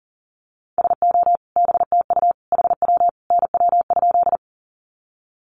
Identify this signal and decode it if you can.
Morse code, heard as a keyed tone, about 40 wpm, 700 Hz. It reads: SO BTU HW NW?
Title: Common QSO Phrases 40wpm